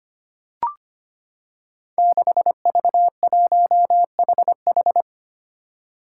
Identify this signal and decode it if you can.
Morse code, heard as a keyed tone, about 25 wpm, 700 Hz. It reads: 6V1HH
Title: Five Challenging Letter-Number Combinations 25wpm